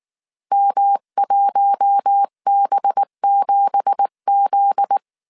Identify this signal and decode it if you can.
Morse code, heard as a keyed tone, about 19 wpm, 800 Hz. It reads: M1B7Z
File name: CW / DET / 1 9 1500C S B 18wpm 2551